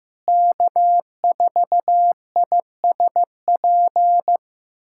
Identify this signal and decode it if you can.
Morse code, heard as a keyed tone, about 15 wpm, 700 Hz. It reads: K4ISP
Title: USA Callsign DE Callsign - 5 Characters - 15wpm